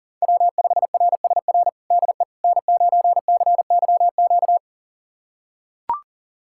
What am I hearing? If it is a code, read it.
Morse code, heard as a keyed tone, about 37 wpm, 700 Hz. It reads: W5RSR DE N9CYQ
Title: USA Callsign DE Callsign - 5 Characters - 40wpm